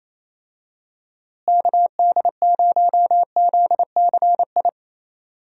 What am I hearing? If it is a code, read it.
Morse code, heard as a keyed tone, about 28 wpm, 700 Hz. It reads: KD0ZCI
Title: USA Call Signs - 6 Characters - Repeated 3x - 28wpm